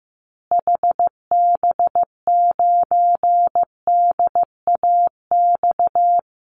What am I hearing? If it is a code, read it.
Morse code, heard as a keyed tone, about 15 wpm, 700 Hz. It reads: HB9DAX
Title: Call Signs Repeated 2x - 15wpm